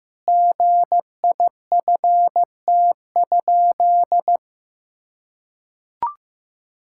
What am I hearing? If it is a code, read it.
Morse code, heard as a keyed tone, about 16 wpm, 700 Hz. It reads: GIFT?
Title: Sentences from Top 1000 Words - Review Part 6 - 15wpm